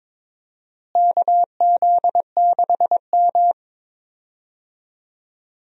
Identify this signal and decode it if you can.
Morse code, heard as a keyed tone, about 22 wpm, 700 Hz. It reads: KZ6M
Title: USA Call Signs - 4 Characters - Repeated 3x - 22wpm